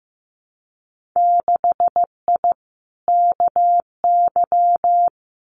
Morse code, heard as a keyed tone, about 15 wpm, 700 Hz. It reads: 6I KY
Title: Winter Field Day Exchanges Encore 15wpm